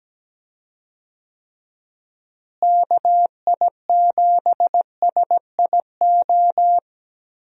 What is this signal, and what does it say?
Morse code, heard as a keyed tone, about 17 wpm, 700 Hz. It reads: KI7SIO
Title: USA Call Signs - 6 Characters - Repeated 2x - 17wpm